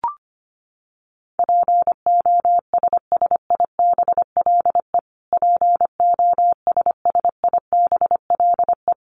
Morse code, heard as a keyed tone, about 25 wpm, 700 Hz. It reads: POSSIBLE POSSIBLE
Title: Top 500 Words Repeated 25wpm